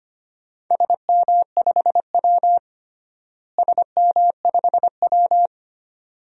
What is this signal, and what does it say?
Morse code, heard as a keyed tone, about 25 wpm, 700 Hz. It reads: SM5W SM5W
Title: DXCC-100 Call Signs - 4 Characters - Repeated 2x - 25wpm